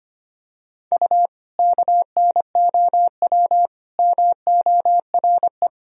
Morse code, heard as a keyed tone, about 25 wpm, 700 Hz. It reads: U KNOW MORE
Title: Sentences from Top 100 Words and Abbreviations 25wpm